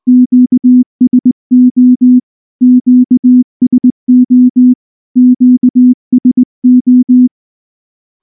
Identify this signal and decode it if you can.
Morse code, heard as a keyed tone, about 18 wpm, 250 Hz. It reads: QSO QSO QSO
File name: morse / morse-training-20WPM-7